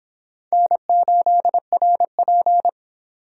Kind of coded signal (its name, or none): Morse code